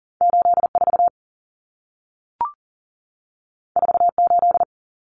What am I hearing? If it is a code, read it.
Morse code, heard as a keyed tone, about 40 wpm, 700 Hz. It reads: 84 48
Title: Common Cut Numbers - 2-Digit 40wpm